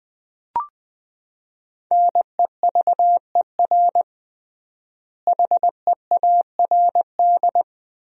Morse code, heard as a keyed tone, about 20 wpm, 700 Hz. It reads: NEVER HEARD
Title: 3x Word-Spacing - Sentences from Top 400 Words - 20wpm